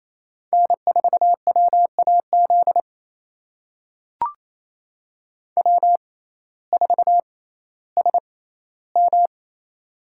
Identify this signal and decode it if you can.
Morse code, heard as a keyed tone, about 28 wpm, spaced as for 15 wpm, 700 Hz. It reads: N4WAZ W4SM
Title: USA Call Signs - 5 Characters - 6x Character Spacing - 28wpm